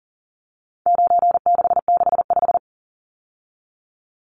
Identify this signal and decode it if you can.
Morse code, heard as a keyed tone, about 40 wpm, 700 Hz. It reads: 9665
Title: Numbers - 4-Digit 40wpm